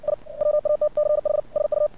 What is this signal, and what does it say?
Morse code, heard as a keyed tone, about 56 wpm, 600 Hz. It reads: I WATCH SU
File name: archived / CRSnet